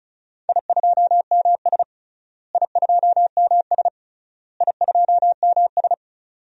Morse code, heard as a keyed tone, about 34 wpm, 700 Hz. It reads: I2MS I2MS I2MS
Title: DXCC-100 Call Signs - 4 Characters - Repeated 3x - 35wpm